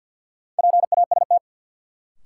Morse code, heard as a keyed tone, about 53 wpm, 700 Hz. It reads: PAST
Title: Sentences from Top 600 Words - New Word Drill - 50wpm